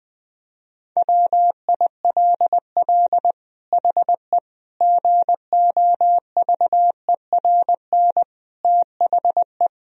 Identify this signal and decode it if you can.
Morse code, heard as a keyed tone, about 21 wpm, 700 Hz. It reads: WILL HE GOVERN THE
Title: Sentences from Top 1000 Words - Review Part 2 - 20wpm